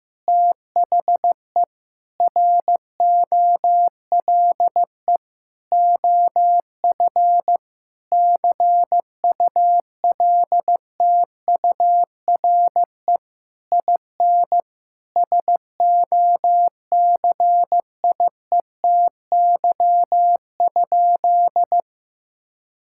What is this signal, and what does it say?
Morse code, heard as a keyed tone, about 15 wpm, 700 Hz. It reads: THE ROLE OF CULTURE IN SOCIETY?
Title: Sentences from Top 1000 Words - Review Part 6 - 15wpm